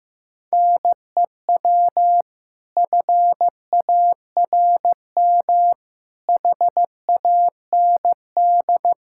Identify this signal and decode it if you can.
Morse code, heard as a keyed tone, about 15 wpm, 700 Hz. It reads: NEW FARM HAND
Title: Pangram Challenge v2 - 15wpm